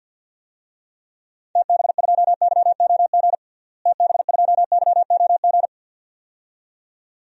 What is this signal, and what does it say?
Morse code, heard as a keyed tone, about 50 wpm, 700 Hz. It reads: TB2YOG TB2YOG
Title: DXCC-100 Call Signs - 6 Characters - Repeated 2x - 50wpm